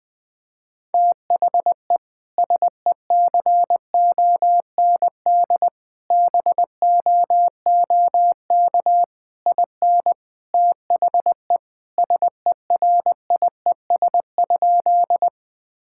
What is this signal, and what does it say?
Morse code, heard as a keyed tone, about 21 wpm, 700 Hz. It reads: THE SECOND BOOK IN THE SERIES?
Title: Sentences from Top 1000 Words - Review Part 5 - 20wpm